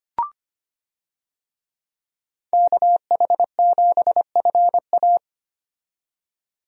Morse code, heard as a keyed tone, about 25 wpm, 700 Hz. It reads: KH7FA
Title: USA Call Signs - 5 Characters - 25wpm